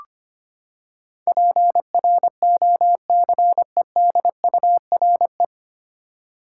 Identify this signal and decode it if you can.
Morse code, heard as a keyed tone, about 25 wpm, 700 Hz. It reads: PROCEDURE
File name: Nine Letter Words - 25wpm